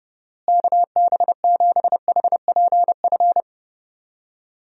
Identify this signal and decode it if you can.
Morse code, heard as a keyed tone, about 32 wpm, 700 Hz. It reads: KB7HPF
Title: USA Call Signs - 6 Characters - Repeated 2x - 30wpm